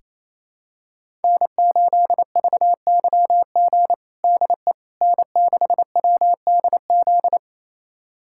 Morse code, heard as a keyed tone, about 28 wpm, 700 Hz. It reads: N8VYG DE N6WDZ
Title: USA Callsign DE Callsign - 5 Characters - 28wpm